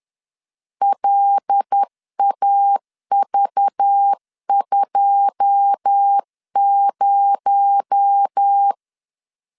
Morse code, heard as a keyed tone, about 10 wpm, 800 Hz. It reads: LAV20
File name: LEZIONI / 10 SIGNS / 10B INT 10wpm 2844